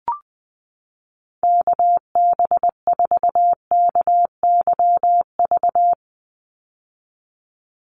Morse code, heard as a keyed tone, about 20 wpm, 700 Hz. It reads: KB4KYV
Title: USA Call Signs 20wpm